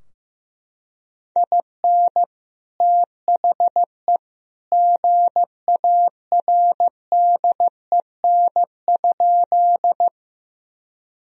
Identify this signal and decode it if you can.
Morse code, heard as a keyed tone, about 16 wpm, 700 Hz. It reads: IN THE GARDEN?
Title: Sentences from Top 1000 Words - Review Part 4 - 15wpm